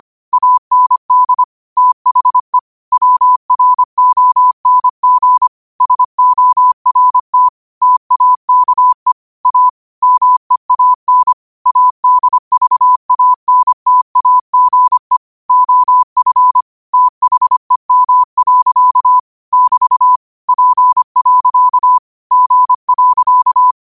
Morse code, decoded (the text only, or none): AND THE WRONG SORT TAKE A MEAN ADVANTAGE OF THEM. <BT> P. G.